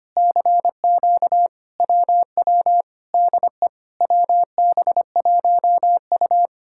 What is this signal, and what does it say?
Morse code, heard as a keyed tone, about 24 wpm, 700 Hz. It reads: CQ WW DE WB1U
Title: Speed-Racing Common QSO Phrases - 25wpm